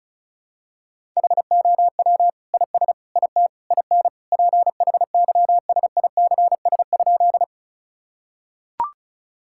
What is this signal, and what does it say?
Morse code, heard as a keyed tone, about 36 wpm, 700 Hz. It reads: HOW IS IT IN PHYSICS?
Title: Sentences from Top 1000 Words - Review Part 5 - 35wpm